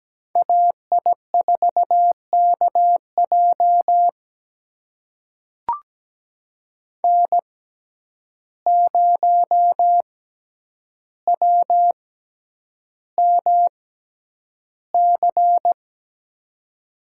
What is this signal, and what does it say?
Morse code, heard as a keyed tone, about 17 wpm, spaced as for 9 wpm, 700 Hz. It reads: AI4KJ N0WMC
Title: USA Call Signs - 5 Characters - 6x Character Spacing - 17wpm